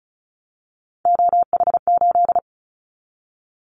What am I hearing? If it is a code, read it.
Morse code, heard as a keyed tone, about 34 wpm, 700 Hz. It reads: OH8
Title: Callsign Prefixes for Top 25 DXCC - 35wpm